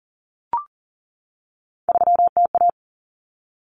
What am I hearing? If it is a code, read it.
Morse code, heard as a keyed tone, about 40 wpm, 700 Hz. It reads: 3TA
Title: Common Cut Numbers - 3-Digit 40wpm